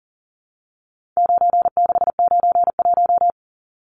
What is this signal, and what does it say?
Morse code, heard as a keyed tone, about 40 wpm, 700 Hz. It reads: 9691
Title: Numbers - 4-Digit 40wpm